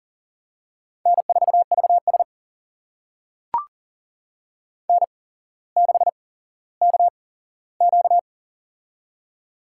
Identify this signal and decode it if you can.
Morse code, heard as a keyed tone, about 39 wpm, spaced as for 17 wpm, 700 Hz. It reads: N4VS N6KQ
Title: USA Call Signs - 4 Characters - 8x Character Spacing - 40wpm